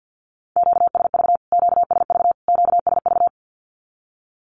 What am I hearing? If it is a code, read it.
Morse code, heard as a keyed tone, about 48 wpm, 700 Hz. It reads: QSV QSV QSV
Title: Q-Codes Repeated 3x 50wpm